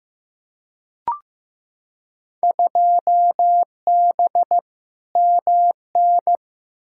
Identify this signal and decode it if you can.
Morse code, heard as a keyed tone, about 15 wpm, 700 Hz. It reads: 2B MN
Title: Field Day Exchanges - 15wpm